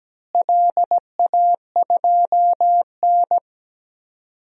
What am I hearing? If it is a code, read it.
Morse code, heard as a keyed tone, about 17 wpm, 700 Hz. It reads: LA2N
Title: DXCC-100 Call Signs - 4 Characters - Repeated 2x - 17wpm